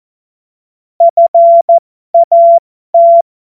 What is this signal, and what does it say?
Morse code, heard as a keyed tone, about 14 wpm, spaced as for 12 wpm, 680 Hz. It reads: FAT
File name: Morsekurs / Witze